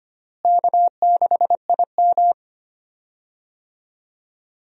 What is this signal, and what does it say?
Morse code, heard as a keyed tone, about 25 wpm, 700 Hz. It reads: K6IM
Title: USA Call Signs - 4 Characters - Repeated 3x - 25wpm